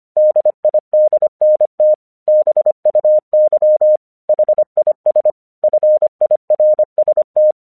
Morse code, heard as a keyed tone, about 25 wpm, 610 Hz. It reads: DIDNT BUY HIS FIRST
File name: 1 English 25wpm / AQC12